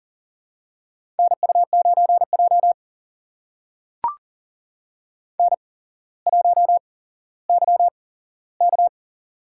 Morse code, heard as a keyed tone, about 37 wpm, spaced as for 17 wpm, 700 Hz. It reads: NU9J N1YK
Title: USA Call Signs - 4 Characters - 8x Character Spacing - 40wpm